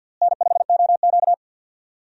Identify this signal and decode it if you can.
Morse code, heard as a keyed tone, about 51 wpm, 700 Hz. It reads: N5OQ
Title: USA Call Signs - 4 Characters - Repeated 2x - 50wpm